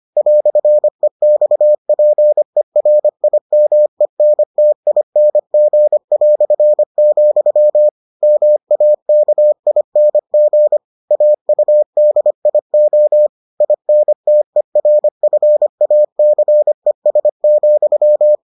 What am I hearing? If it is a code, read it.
Morse code, heard as a keyed tone, about 25 wpm, 600 Hz. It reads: "EXPERIMENTING", MAKING AUDIO INTERFACES,